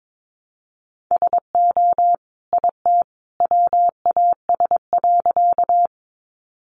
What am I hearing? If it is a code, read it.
Morse code, heard as a keyed tone, about 22 wpm, 700 Hz. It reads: SO IT WAS.
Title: Sentences from Top 100 Words 22wpm